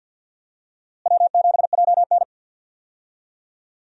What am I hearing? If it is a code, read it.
Morse code, heard as a keyed tone, about 52 wpm, 700 Hz. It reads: W7JN
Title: USA Call Signs - 4 Characters - Repeated 3x - 50wpm